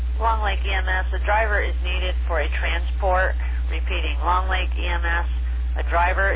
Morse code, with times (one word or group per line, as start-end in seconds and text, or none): none